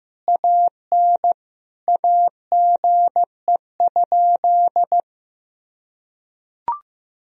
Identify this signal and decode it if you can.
Morse code, heard as a keyed tone, about 15 wpm, 700 Hz. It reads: AN AGE?
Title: Sentences from Top 1000 Words - Review Part 3 - 15wpm